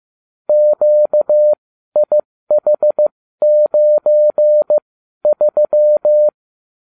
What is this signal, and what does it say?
Morse code, heard as a keyed tone, about 15 wpm, spaced as for 12 wpm, 600 Hz. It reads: QIH93